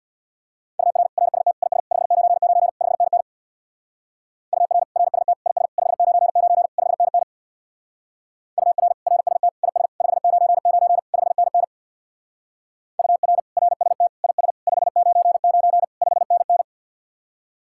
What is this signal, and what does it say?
Morse code, heard as a keyed tone, about 62 wpm, 700 Hz. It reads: UR RST IS 599 5NN UR RST IS 599 5NN UR RST IS 599 5NN UR RST IS 599 5NN
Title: Speed-Racing Common QSO Phrases Encore - 50wpm